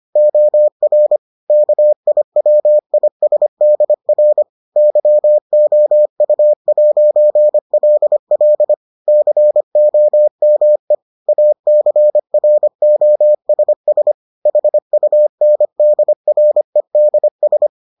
Morse code, heard as a keyed tone, about 25 wpm, 600 Hz. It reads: OR KIWISDR YOU'LL COME ACROSS HUNDREDS